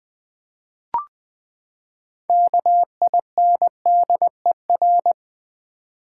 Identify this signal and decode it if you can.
Morse code, heard as a keyed tone, about 20 wpm, 700 Hz. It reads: KINDER
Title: Build A Word - 6 Letters At A Time - 12-Letter Words - 20wpm